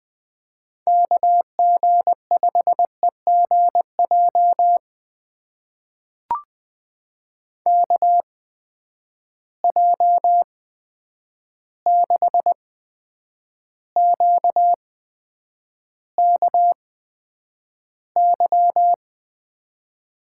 Morse code, heard as a keyed tone, about 20 wpm, spaced as for 9 wpm, 700 Hz. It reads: KG5EGJ KJ6QKY
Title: USA Call Signs - 6 Characters - 8x Character Spacing - 20wpm